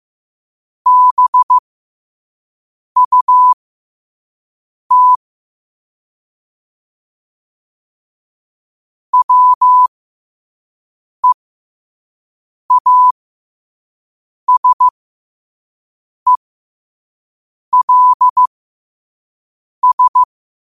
Morse code, heard as a keyed tone, about 15 wpm, spaced as for 5 wpm, 1000 Hz. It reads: BUT WEASELS
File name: qotdfiles / 5wpm